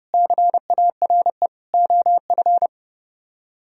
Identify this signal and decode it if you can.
Morse code, heard as a keyed tone, about 28 wpm, 700 Hz. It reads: CARE OF